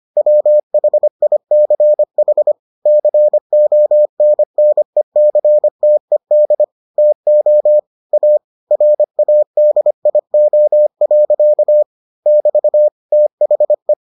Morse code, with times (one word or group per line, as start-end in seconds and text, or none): 0.2-2.5 WHICH
2.9-6.6 CONNECTED
7.0-7.8 TO
8.1-8.4 A
8.7-11.8 RADIO.
12.3-12.9 <BT>
13.1-13.9 THE